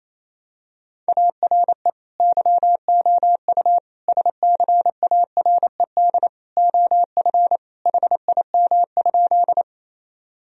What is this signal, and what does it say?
Morse code, heard as a keyed tone, about 27 wpm, 700 Hz. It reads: ARE YOU SCARED OF HIM?
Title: Sentences from Top 1000 Words - Review Part 4 - 28wpm